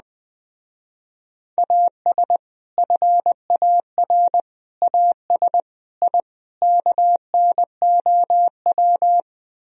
Morse code, heard as a keyed tone, about 20 wpm, 700 Hz. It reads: AS FAR AS I KNOW
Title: Speed-Racing - Sets of 5 Words 17wpm